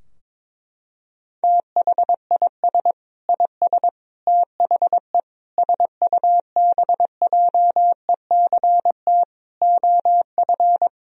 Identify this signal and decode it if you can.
Morse code, heard as a keyed tone, about 22 wpm, 700 Hz. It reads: THIS IS THE SUBJECT OF
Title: Sentences from Top 1000 Words - Review Part 4 - 22wpm